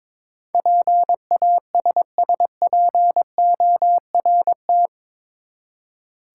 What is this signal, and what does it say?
Morse code, heard as a keyed tone, about 22 wpm, 700 Hz. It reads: PASSPORT
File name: practice / 22wpm / Eight Letter Words - 22wpm